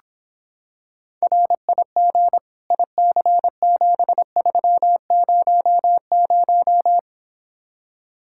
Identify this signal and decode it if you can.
Morse code, heard as a keyed tone, about 26 wpm, 700 Hz. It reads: RIG IC7300